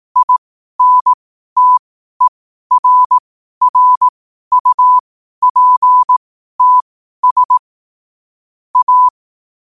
Morse code, heard as a keyed tone, about 18 wpm, spaced as for 12 wpm, 1000 Hz. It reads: INTERRUPTS A
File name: qotdfiles / 12wpm